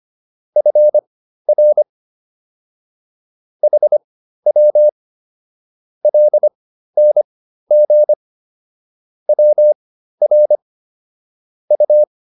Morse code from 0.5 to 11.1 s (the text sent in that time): FR HW LNG WR